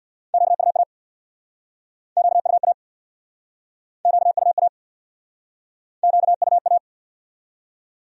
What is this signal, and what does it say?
Morse code, heard as a keyed tone, about 67 wpm, 700 Hz. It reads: QUA QUA QUA QUA